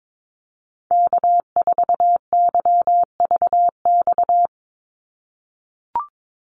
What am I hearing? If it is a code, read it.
Morse code, heard as a keyed tone, about 22 wpm, 700 Hz. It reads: K4YVX
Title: USA Call Signs 22wpm